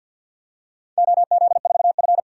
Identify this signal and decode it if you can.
Morse code, heard as a keyed tone, about 51 wpm, 700 Hz. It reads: OZ4F
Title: DXCC-100 Call Signs - 4 Characters - 50wpm